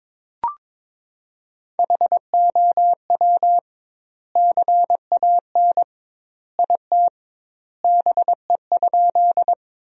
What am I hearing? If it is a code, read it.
Morse code, heard as a keyed tone, about 21 wpm, 700 Hz. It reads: HOW CAN IT BE?
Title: 2x Word-Spacing - Sentences from Top 100 Words - 22wpm